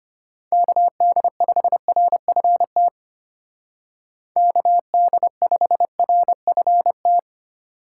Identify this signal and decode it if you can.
Morse code, heard as a keyed tone, about 31 wpm, 700 Hz. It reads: KD5RFT KD5RFT
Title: Speed Racing - USA 2x3 Call Signs 25wpm